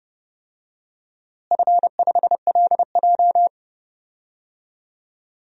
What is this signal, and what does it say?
Morse code, heard as a keyed tone, about 30 wpm, 700 Hz. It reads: F5LJ
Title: DXCC-100 Call Signs - 4 Characters - Repeated 2x - 30wpm